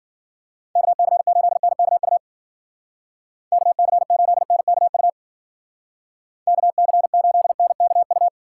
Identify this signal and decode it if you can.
Morse code, heard as a keyed tone, about 52 wpm, 700 Hz. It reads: KC8NKU KC8NKU KC8NKU